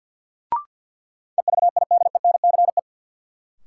Morse code, heard as a keyed tone, about 52 wpm, 700 Hz. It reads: EVIDENCE
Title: Sentences from Top 1300 Words - New Word Drill Alt - 50wpm